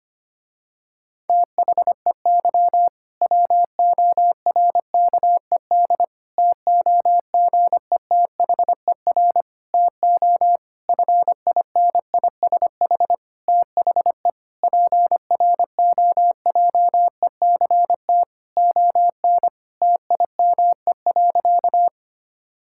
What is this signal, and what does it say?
Morse code, heard as a keyed tone, about 25 wpm, 700 Hz. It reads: THEY WORKED TOGETHER TO FINISH THE PROJECT ON TIME.
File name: practice / 25wpm / Difficult Sentences with Hints - 25wpm